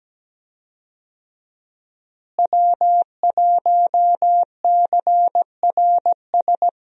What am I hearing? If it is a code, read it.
Morse code, heard as a keyed tone, about 17 wpm, 700 Hz. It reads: W1CRS
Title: USA Call Signs - 5 Characters - Repeated 3x - 17wpm